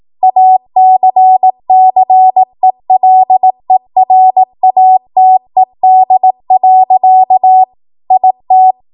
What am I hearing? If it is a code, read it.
Morse code, heard as a keyed tone, about 18 wpm, 750 Hz. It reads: ACCELERATED. IT
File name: Archive / 18 WPM / 18WPM